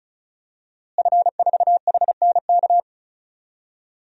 Morse code, heard as a keyed tone, about 36 wpm, 700 Hz. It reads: F4HNK
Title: DXCC-100 Call Signs - 5 Characters - Repeated 2x - 35wpm